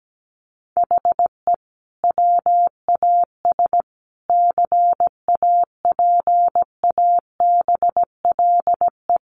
Morse code, heard as a keyed tone, about 17 wpm, 700 Hz. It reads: HE WAS CAPABLE
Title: Sentences from Top 1200 Words 17wpm